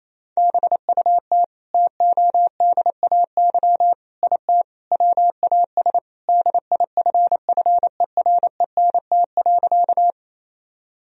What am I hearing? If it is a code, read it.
Morse code, heard as a keyed tone, about 28 wpm, 700 Hz. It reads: BUT TODAY IT WAS DIFFERENT.
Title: Sentences from Top 1000 Words - Review Part 4 - 28wpm